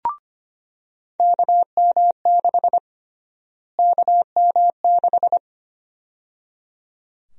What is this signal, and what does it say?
Morse code, heard as a keyed tone, about 25 wpm, 700 Hz. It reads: KM6 KM6
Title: USA Call Sign Prefixes - Repeated 2x - 25wpm